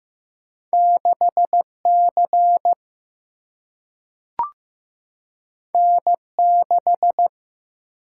Morse code, heard as a keyed tone, about 15 wpm, 700 Hz. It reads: 6 C N 6
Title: Single-Letter Single-Number 15wpm